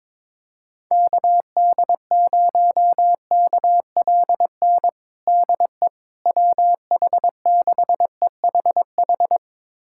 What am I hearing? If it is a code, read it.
Morse code, heard as a keyed tone, about 22 wpm, 700 Hz. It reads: KD0KLN DE WH6EHH
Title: USA Callsign DE Callsign - 6 Characters - 22wpm